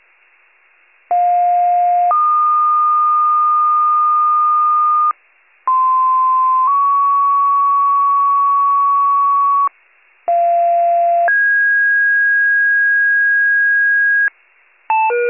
Two-tone page, heard as a one-second tone, then a three-second tone, about 710 then 1220 Hz, about 1040 then 1120 Hz, about 690 then 1650 Hz. It is heard from 1.1 to 14.3 s.